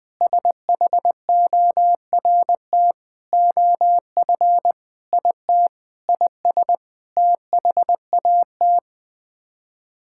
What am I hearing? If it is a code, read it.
Morse code, heard as a keyed tone, about 20 wpm, 700 Hz. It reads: SHORT OF IT IS THAT
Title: Sets of 8 Words - 20wpm